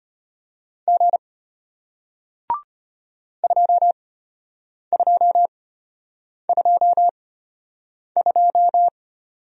Morse code, heard as a keyed tone, about 33 wpm, spaced as for 8 wpm, 700 Hz. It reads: G 2222